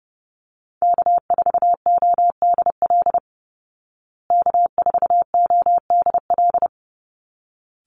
Morse code, heard as a keyed tone, about 28 wpm, 700 Hz. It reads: K4ODL K4ODL
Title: Call Signs Repeated 3x - 30wpm